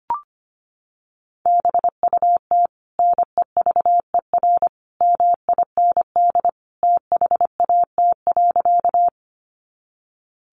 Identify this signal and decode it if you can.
Morse code, heard as a keyed tone, about 25 wpm, 700 Hz. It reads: BUT NEVER MIND THAT.
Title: Sentences from Top 500 Words 25wpm